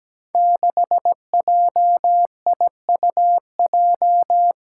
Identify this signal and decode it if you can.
Morse code, heard as a keyed tone, about 17 wpm, 700 Hz. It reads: 6JIUJ
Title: Five Challenging Letter-Number Combinations 17wpm